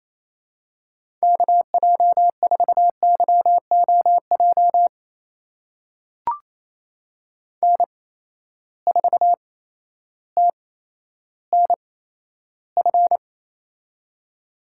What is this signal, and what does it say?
Morse code, heard as a keyed tone, about 28 wpm, spaced as for 13 wpm, 700 Hz. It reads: KJ4YOJ N4TNF
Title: USA Call Signs - 8x Character Spacing - 28wpm